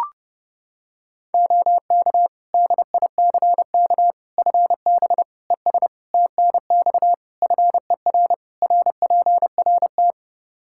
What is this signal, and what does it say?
Morse code, heard as a keyed tone, about 30 wpm, 700 Hz. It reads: OK DICK FB ES TNX FER RPRT
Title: Practice QSO - Exchange 2 - Info - 30wpm